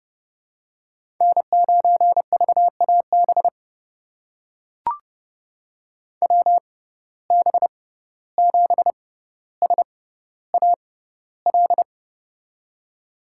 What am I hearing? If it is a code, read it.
Morse code, heard as a keyed tone, about 31 wpm, spaced as for 15 wpm, 700 Hz. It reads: N9VAB WB7SAL
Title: USA Call Signs - 6x Character Spacing - 30wpm